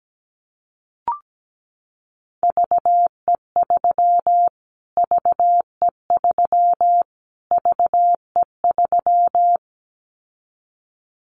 Morse code, heard as a keyed tone, about 17 wpm, 700 Hz. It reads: VE3 VE3 VE3
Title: Callsign Prefixes for Top 25 DXCC - Repeated 3x - 17wpm